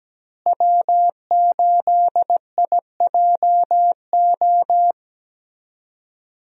Morse code, heard as a keyed tone, about 17 wpm, 700 Hz. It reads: W8IJO
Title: USA Call Signs - 5 Characters - Repeated 3x - 17wpm